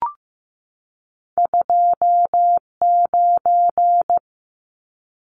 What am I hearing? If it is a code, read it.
Morse code, heard as a keyed tone, about 15 wpm, 700 Hz. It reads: 29
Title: Numbers - 2-Digit 15wpm